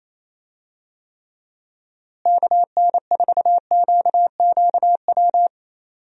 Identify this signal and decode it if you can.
Morse code, heard as a keyed tone, about 28 wpm, 700 Hz. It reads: KN4QQW
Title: USA Call Signs - 6 Characters - Repeated 2x - 28wpm